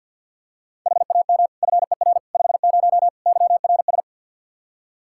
Morse code, heard as a keyed tone, about 52 wpm, 700 Hz. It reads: HAM FER 50 YRS
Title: Practice QSO - Exchange 3 - More Info - 50wpm